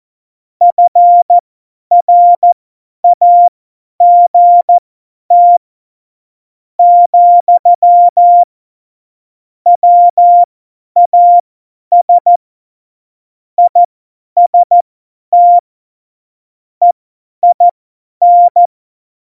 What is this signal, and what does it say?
Morse code, heard as a keyed tone, about 14 wpm, spaced as for 9 wpm, 700 Hz. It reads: FRAGT, WAS IST EIN
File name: Morsekurs / Witze / Witz 11. Juni 2019, 50 F70